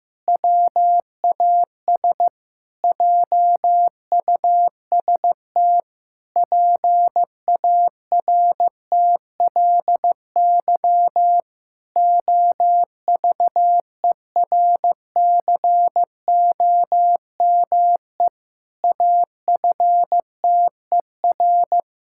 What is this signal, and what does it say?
Morse code, heard as a keyed tone, about 15 wpm, 700 Hz. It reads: WAS JUST PARTLY OVERCOME AFTER
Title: Pangram Challenge v2 - 15wpm